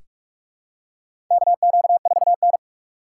Morse code, heard as a keyed tone, about 46 wpm, 700 Hz. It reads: KQ4N
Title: USA Call Signs - 4 Characters - 3x Character Spacing - 45wpm